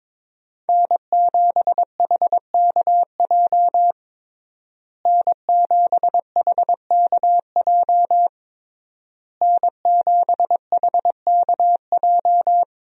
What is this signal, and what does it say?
Morse code, heard as a keyed tone, about 21 wpm, 700 Hz. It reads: N7HKJ N7HKJ N7HKJ
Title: USA Call Signs - 5 Characters - Repeated 3x - 22wpm